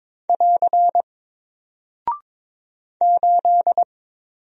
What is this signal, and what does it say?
Morse code, heard as a keyed tone, about 22 wpm, 700 Hz. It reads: <AR> 8
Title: Single Character (Letter:Number:Punctuation:Prosign) Alt 22wpm